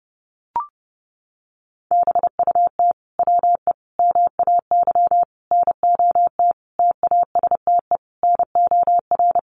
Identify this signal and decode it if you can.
Morse code, heard as a keyed tone, about 30 wpm, 700 Hz. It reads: BUT WE MAY NOT TASTE NOR
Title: Sentences from Top 1500 Words 30wpm